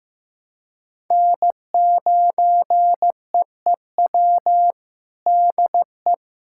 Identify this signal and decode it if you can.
Morse code, heard as a keyed tone, about 15 wpm, 700 Hz. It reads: N9EEW DE
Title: USA Callsign DE Callsign - 5 Characters - 15wpm